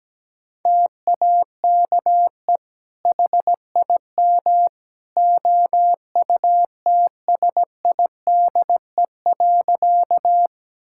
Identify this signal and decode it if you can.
Morse code, heard as a keyed tone, about 17 wpm, 700 Hz. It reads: TAKE HIM OUTSIDE.
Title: Sentences from Top 1000 Words - Review Part 6 - 17wpm